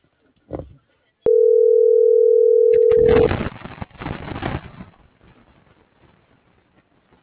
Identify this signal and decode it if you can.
Ringback tone (North America), 440 + 480 Hz.
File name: CENTRAL TERRESTRE / out claudiacruz - .71917